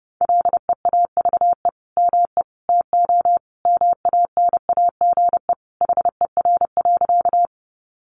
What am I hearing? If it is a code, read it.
Morse code, heard as a keyed tone, about 30 wpm, 700 Hz. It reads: LEAVE ME TO MANAGE HER.
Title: Sentences from Top 1400 Words 30wpm